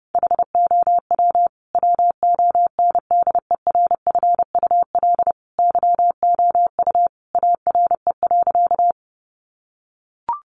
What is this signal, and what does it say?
Morse code, heard as a keyed tone, about 30 wpm, 700 Hz. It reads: HOW WONDERFUL YOU ARE.
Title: Sentences from Top 1100 Words 30wpm